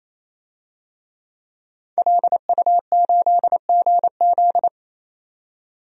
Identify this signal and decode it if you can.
Morse code, heard as a keyed tone, about 27 wpm, 700 Hz. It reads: LU8GZ
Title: DXCC-100 Call Signs - 5 Characters - 28wpm